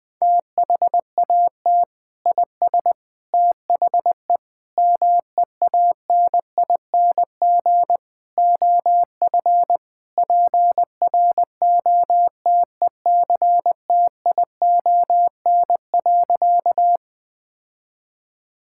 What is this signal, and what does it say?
Morse code, heard as a keyed tone, about 21 wpm, 700 Hz. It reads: THAT IS THE MEANING OF PROTECTION.
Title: Sentences from Top 1600 Words 20wpm